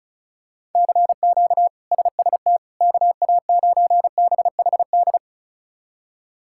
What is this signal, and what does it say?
Morse code, heard as a keyed tone, about 36 wpm, 700 Hz. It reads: CQ SST KA9BHD
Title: K1USN SST QSO Practice - 35wpm